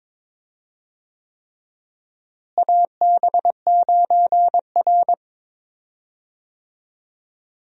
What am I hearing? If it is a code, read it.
Morse code, heard as a keyed tone, about 22 wpm, 700 Hz. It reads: AB9R
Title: USA Call Signs - 4 Characters - Repeated 3x - 22wpm